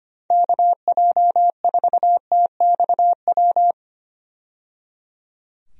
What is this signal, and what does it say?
Morse code, heard as a keyed tone, about 25 wpm, 700 Hz. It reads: KJ4TXW
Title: USA Call Signs - 6 Characters - 25wpm